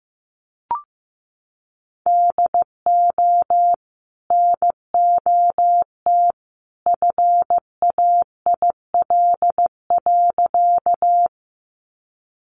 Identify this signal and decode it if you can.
Morse code, heard as a keyed tone, about 15 wpm, 700 Hz. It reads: DO NOT FAIL.
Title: Sentences from Top 1100 Words 15wpm